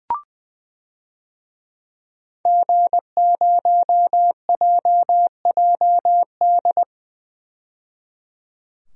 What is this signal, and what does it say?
Morse code, heard as a keyed tone, about 21 wpm, 700 Hz. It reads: G0JJD
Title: DXCC-100 Call Signs - 5 Characters - 20wpm